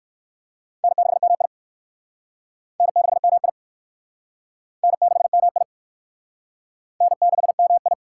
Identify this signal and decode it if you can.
Morse code, heard as a keyed tone, about 58 wpm, 700 Hz. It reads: N6MI N6MI N6MI N6MI